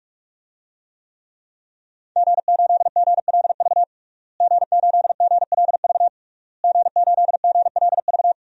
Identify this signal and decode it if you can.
Morse code, heard as a keyed tone, about 46 wpm, 700 Hz. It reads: G8GLV G8GLV G8GLV
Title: DXCC-100 Call Signs - 5 Characters - Repeated 3x - 45wpm